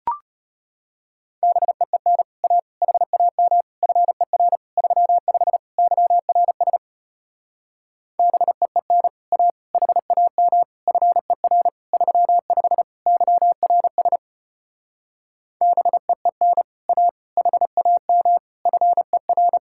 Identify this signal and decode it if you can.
Morse code, heard as a keyed tone, about 33 wpm, 700 Hz. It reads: BEEN A HAM FER 35 YRS BEEN A HAM FER 35 YRS BEEN A HAM FER